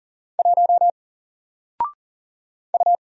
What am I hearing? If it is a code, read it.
Morse code, heard as a keyed tone, about 40 wpm, 700 Hz. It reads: 1 U
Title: Single Letter-Number 40wpm